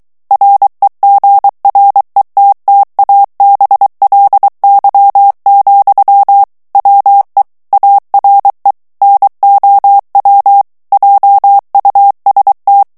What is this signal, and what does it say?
Morse code, heard as a keyed tone, about 23 wpm, 800 Hz. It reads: REGRETTABLY, WE ARE NOW JUST
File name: morse mp3 / ANGLAIS texte clair 2 vitesse 22 mots minute